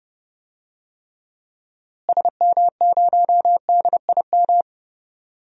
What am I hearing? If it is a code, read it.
Morse code, heard as a keyed tone, about 30 wpm, 700 Hz. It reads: SM0DIM
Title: DXCC-100 Call Signs - 6 Characters - Repeated 2x - 30wpm